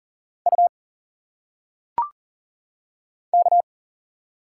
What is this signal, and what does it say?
Morse code, heard as a keyed tone, about 40 wpm, 700 Hz. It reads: U K
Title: Single Letters - Phonetic Pronunciation 40wpm